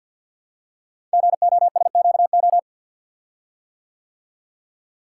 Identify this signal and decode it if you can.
Morse code, heard as a keyed tone, about 52 wpm, 700 Hz. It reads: GOSQO
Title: Code Groups - 5 Letters - 50wpm